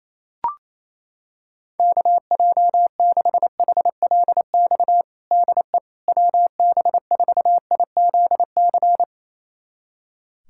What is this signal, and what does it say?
Morse code, heard as a keyed tone, about 28 wpm, 700 Hz. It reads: KJ6HLX DE WB4IZC
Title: USA Callsign DE Callsign - 6 Characters - 28wpm